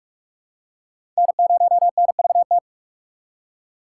Morse code, heard as a keyed tone, about 46 wpm, 700 Hz. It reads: N0NVT
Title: USA Call Signs - 5 Characters - 2x Character Spacing - 45wpm